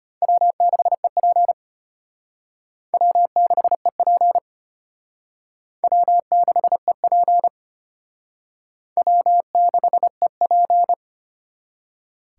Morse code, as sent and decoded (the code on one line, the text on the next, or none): .-- -.... .--. / .-- -.... .--. / .-- -.... .--. / .-- -.... .--.
W6EP W6EP W6EP W6EP